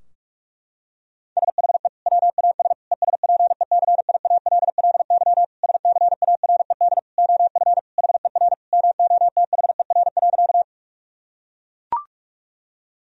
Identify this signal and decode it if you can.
Morse code, heard as a keyed tone, about 41 wpm, 700 Hz. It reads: SHE WAS ESPECIALLY SCARED OF HER MOTHER.
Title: Sentences from Top 1000 Words - Review Part 4 - 45wpm